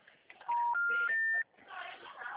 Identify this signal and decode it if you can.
Special information tone, with speech.